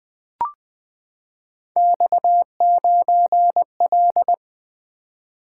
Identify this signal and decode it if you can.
Morse code, heard as a keyed tone, about 20 wpm, 700 Hz. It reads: X9L
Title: Three Letter-Number Combinations - 20wpm